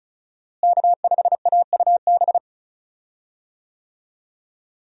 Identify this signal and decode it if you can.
Morse code, heard as a keyed tone, about 36 wpm, 700 Hz. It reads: K5AUB
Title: USA Call Signs - 5 Characters - Repeated 2x - 35wpm